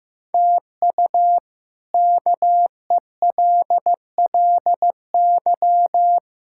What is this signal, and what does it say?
Morse code, heard as a keyed tone, about 16 wpm, 700 Hz. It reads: TU KELLY
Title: K1USN SST QSO Practice - 15wpm